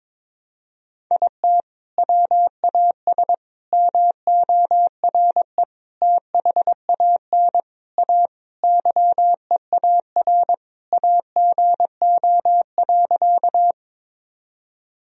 Morse code, heard as a keyed tone, about 22 wpm, 700 Hz. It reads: IT WAS MORE THAN A YEAR AGO.
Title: Sentences from Top 1000 Words - Review Part 3 - 22wpm